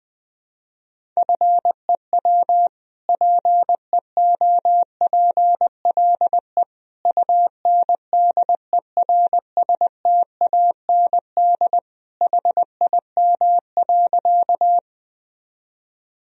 Morse code, heard as a keyed tone, about 20 wpm, 700 Hz. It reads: FEW PEOPLE UNDERSTAND HIM.
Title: Sentences from Top 1000 Words - Review Part 1 - 20wpm